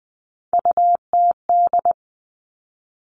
Morse code, heard as a keyed tone, about 20 wpm, 700 Hz. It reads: UTD
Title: Three Challenging Letter-Number Combinations 20wpm